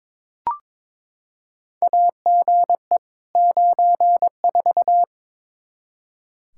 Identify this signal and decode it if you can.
Morse code, heard as a keyed tone, about 22 wpm, 700 Hz. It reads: AGE 94
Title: K1USN SST QSO Practice 22wpm